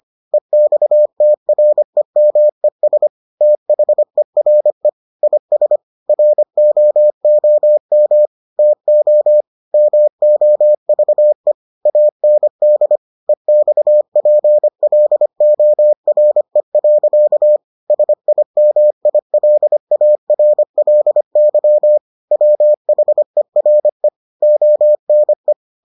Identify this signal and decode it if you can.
Morse code, heard as a keyed tone, about 25 wpm, 600 Hz. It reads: EXTREMES THERE IS ROOM TO MOVE AND EXPLORE. SIMILARLY WHERE ONE